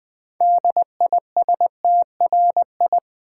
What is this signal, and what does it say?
Morse code, heard as a keyed tone, about 20 wpm, 700 Hz. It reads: DISTRI
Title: Build A Word - 6 Letters At A Time - 12-Letter Words - 20wpm